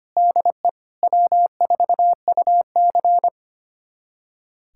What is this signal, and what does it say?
Morse code, heard as a keyed tone, about 25 wpm, 700 Hz. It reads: DE W4UC
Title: USA Callsign DE Callsign - 4 Characters - 25wpm